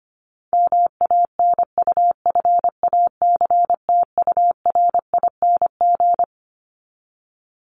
Morse code, heard as a keyed tone, about 25 wpm, 700 Hz. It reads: MANUFACTURING
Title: ING Suffix - Top 100 Words with Base Words 25wpm